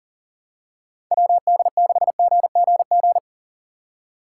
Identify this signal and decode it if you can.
Morse code, heard as a keyed tone, about 39 wpm, 700 Hz. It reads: WD6GGG
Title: USA Call Signs - 6 Characters - 4x Character Spacing - 40wpm